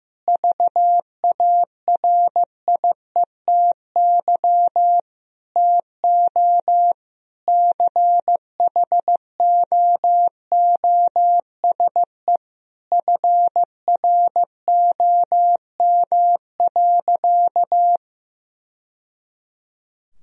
Morse code, heard as a keyed tone, about 15 wpm, 700 Hz. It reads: VARIETY TO CHOOSE FROM.
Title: Sentences from Top 1000 Words - Review Part 4 - 15wpm